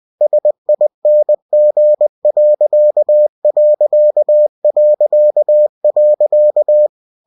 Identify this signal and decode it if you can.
Morse code, heard as a keyed tone, about 20 wpm, 600 Hz. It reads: SING....